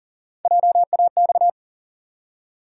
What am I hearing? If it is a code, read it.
Morse code, heard as a keyed tone, about 40 wpm, 700 Hz. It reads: JAX
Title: USA Call Sign Postfixes - 3 Letters 40wpm